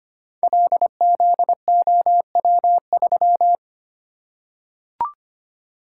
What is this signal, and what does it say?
Morse code, heard as a keyed tone, about 25 wpm, 700 Hz. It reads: LZOW3
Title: Code Groups - 5 Letter:Numbers - 25wpm